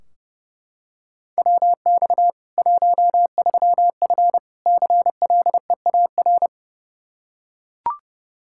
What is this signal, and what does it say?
Morse code, heard as a keyed tone, about 30 wpm, 700 Hz. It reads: WX 13F CLEAR
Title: K1USN SST QSO Practice 30wpm